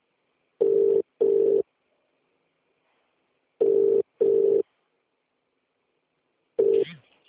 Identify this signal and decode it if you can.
Ringback tone (UK), 400 + 450 Hz.